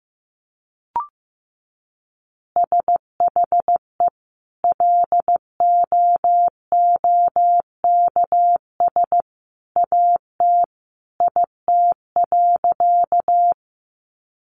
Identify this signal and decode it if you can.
Morse code, heard as a keyed tone, about 15 wpm, 700 Hz. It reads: SHE LOOKS AT IT.
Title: Additional Easy Phrases 15wpm